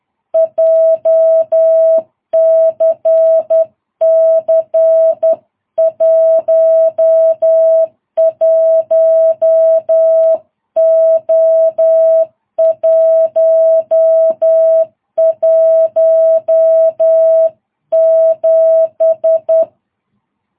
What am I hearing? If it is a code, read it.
Morse code, heard as a keyed tone, about 11 wpm, 660 Hz. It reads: JCC 11O117
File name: jh8cbh / 2022cw